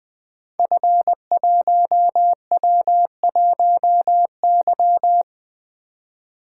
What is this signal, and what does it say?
Morse code, heard as a keyed tone, about 20 wpm, 700 Hz. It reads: F1W1Y
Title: Code Groups - 5 Letter:Numbers - 20wpm